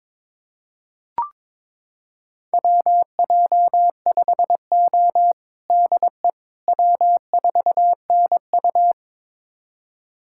Morse code, heard as a keyed tone, about 22 wpm, 700 Hz. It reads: WJ5O DE W4NU
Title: USA Callsign DE Callsign - 4 Characters - 22wpm